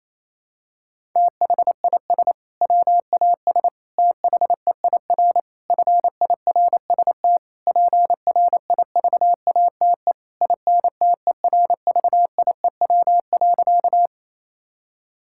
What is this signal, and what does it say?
Morse code, heard as a keyed tone, about 28 wpm, 700 Hz. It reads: THIS WAS THEIR FIRST PRIVATE INTERVIEW.
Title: Sentences from Top 1600 Words 28wpm